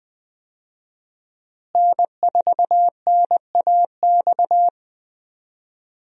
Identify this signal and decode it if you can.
Morse code, heard as a keyed tone, about 20 wpm, 700 Hz. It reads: N4NAX
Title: USA Call Signs - 5 Characters - Repeated 3x - 20wpm